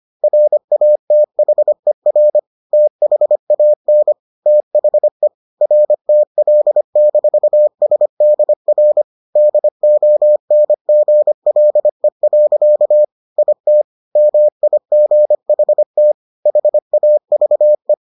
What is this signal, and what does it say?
Morse code, heard as a keyed tone, about 25 wpm, 600 Hz. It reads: RATHER THAN THE RTL-SDR DONGLE. IT MIGHT HAVE